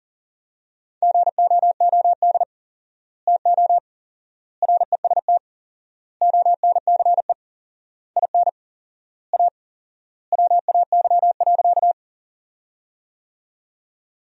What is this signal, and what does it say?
Morse code, heard as a keyed tone, about 40 wpm, 700 Hz. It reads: GOOD TO REST ONCE IN A WAY.
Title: 4x Word-Spacing - Sentences from Top 500 Words - 40wpm